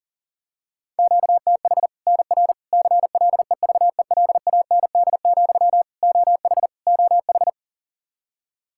Morse code, heard as a keyed tone, about 40 wpm, 700 Hz. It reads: QTH NR CLEVELAND, OH OH
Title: K1USN SST QSO Practice 40wpm